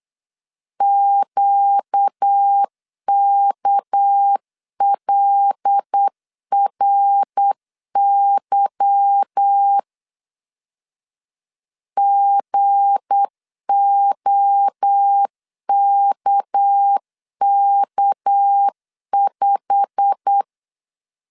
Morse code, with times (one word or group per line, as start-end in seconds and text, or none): 0.8-9.8 QKLRY
12.0-20.4 GOKK5